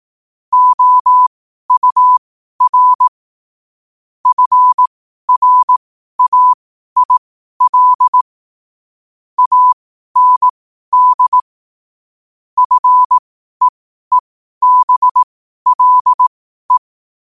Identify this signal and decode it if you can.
Morse code, heard as a keyed tone, about 18 wpm, spaced as for 12 wpm, 1000 Hz. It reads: OUR FRAIL AND FEEBLE
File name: 12wpm